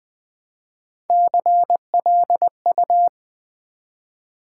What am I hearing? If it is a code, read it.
Morse code, heard as a keyed tone, about 20 wpm, 700 Hz. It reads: CLU
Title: Build A Word - 1 Letter At A Time - 4-Letter Words - 20wpm